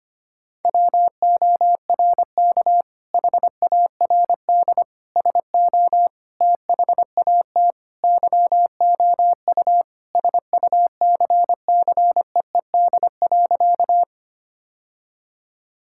Morse code, heard as a keyed tone, about 25 wpm, 700 Hz. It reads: WORK HARD SO THAT YOU SUCCEED.
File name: practice / 25wpm / Sentences with 2-Word Phrases - 25wpm